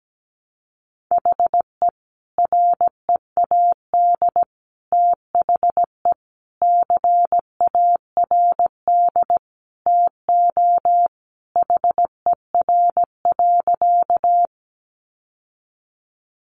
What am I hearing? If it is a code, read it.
Morse code, heard as a keyed tone, about 17 wpm, 700 Hz. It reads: HE READ THE CARD TO HER.
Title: Sentences from Top 600 Words 17wpm